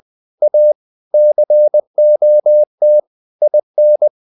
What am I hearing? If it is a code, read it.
Morse code, heard as a keyed tone, about 20 wpm, 600 Hz. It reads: A COT IN
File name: the old man and the sea / the old man and the sea part 1 20wpm